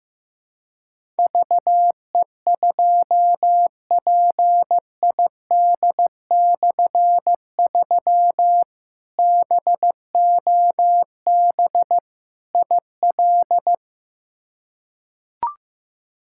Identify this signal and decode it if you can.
Morse code, heard as a keyed tone, about 16 wpm, 700 Hz. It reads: VE2PID/3 BOB IL
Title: K1USN SST QSO Practice - 15wpm